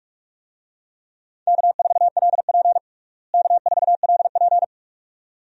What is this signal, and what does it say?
Morse code, heard as a keyed tone, about 46 wpm, 700 Hz. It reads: K4LP K4LP
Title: USA Call Signs - 4 Characters - Repeated 2x - 45wpm